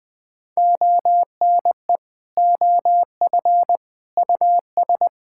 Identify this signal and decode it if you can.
Morse code, heard as a keyed tone, about 20 wpm, 700 Hz. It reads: ONE OF US
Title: Sentences from Top 1600 Words 20wpm